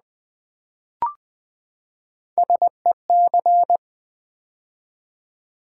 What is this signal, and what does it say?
Morse code, heard as a keyed tone, about 20 wpm, 700 Hz. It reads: SEC
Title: Build A Word - 3 Letters At A Time - 8-Letter Words - 20wpm